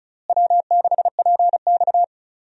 Morse code, heard as a keyed tone, about 36 wpm, 700 Hz. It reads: W6PX
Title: USA Call Signs - 4 Characters - Repeated 3x - 35wpm